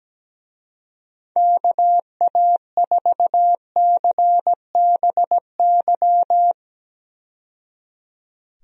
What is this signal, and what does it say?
Morse code, heard as a keyed tone, about 17 wpm, 700 Hz. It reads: KA4CBY
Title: USA Call Signs - 6 Characters - 17wpm